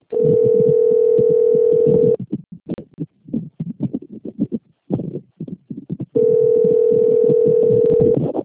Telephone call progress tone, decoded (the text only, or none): ringback tone